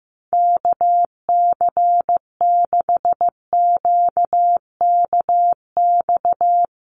Morse code, heard as a keyed tone, about 15 wpm, 700 Hz. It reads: KC6QKX
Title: USA Call Signs Repeated 3x 15wpm